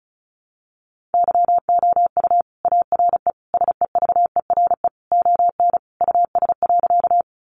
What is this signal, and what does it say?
Morse code, heard as a keyed tone, about 34 wpm, 700 Hz. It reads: YOU ARE SEVERE ON US.
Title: Sentences from Top 1400 Words 35wpm